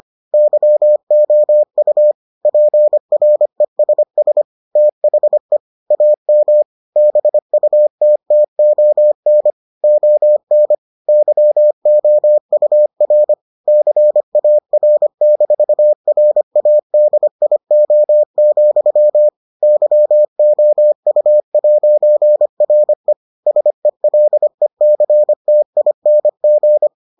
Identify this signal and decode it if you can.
Morse code, heard as a keyed tone, about 25 wpm, 600 Hz. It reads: YOU PRESS THE AM BUTTON ON YOUR CAR-RADIO, YOU'RE SELECTING